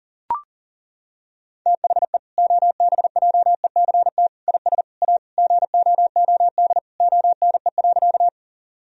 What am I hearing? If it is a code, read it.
Morse code, heard as a keyed tone, about 40 wpm, 700 Hz. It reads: THE OBJECT IS A GOOD ONE.
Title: Sentences from Top 1000 Words - Review Part 3 - 40wpm